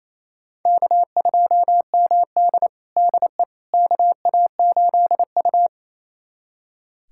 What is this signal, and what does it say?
Morse code, heard as a keyed tone, about 28 wpm, 700 Hz. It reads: K2MD DE KA8U
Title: USA Callsign DE Callsign - 4 Characters - 28wpm